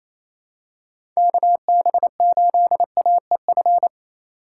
Morse code, heard as a keyed tone, about 28 wpm, 700 Hz. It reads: KB8AEF
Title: USA Call Signs - 6 Characters - 2x Character Spacing - 28wpm